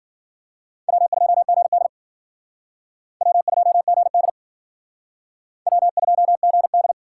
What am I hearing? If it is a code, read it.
Morse code, heard as a keyed tone, about 53 wpm, 700 Hz. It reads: W2GD W2GD W2GD